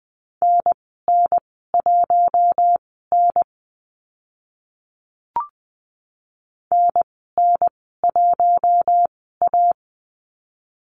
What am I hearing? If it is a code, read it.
Morse code, heard as a keyed tone, about 20 wpm, spaced as for 15 wpm, 700 Hz. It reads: NN1N NN1A
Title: 042 Lesson 8g - Number 1 - Call Signs 20wpm